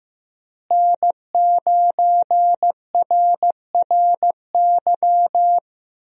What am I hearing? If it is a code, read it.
Morse code, heard as a keyed tone, about 16 wpm, 700 Hz. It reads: N9RRY
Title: USA Call Signs - 5 Characters - Repeated 2x - 15wpm